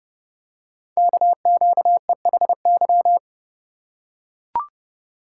Morse code, heard as a keyed tone, about 30 wpm, 700 Hz. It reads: KQEHY
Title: Code Groups - 5 Letters - 30wpm